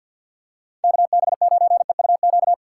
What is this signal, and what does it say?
Morse code, heard as a keyed tone, about 49 wpm, 700 Hz. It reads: KB9EUQ
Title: USA Call Signs - 2x Character Spacing - 50wpm